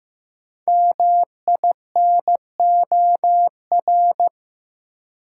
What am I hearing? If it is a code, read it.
Morse code, heard as a keyed tone, about 15 wpm, 700 Hz. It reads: MINOR
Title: Sentences from Top 1500 Words - New Word Drill - 15wpm